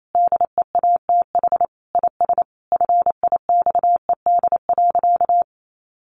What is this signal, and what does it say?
Morse code, heard as a keyed tone, about 28 wpm, 700 Hz. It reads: DEATH IS FIXED.
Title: Sentences from Top 1100 Words 28wpm